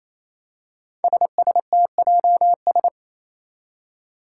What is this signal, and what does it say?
Morse code, heard as a keyed tone, about 28 wpm, 700 Hz. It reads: SSTJS
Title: Five Challenging Letter-Number Combinations 28wpm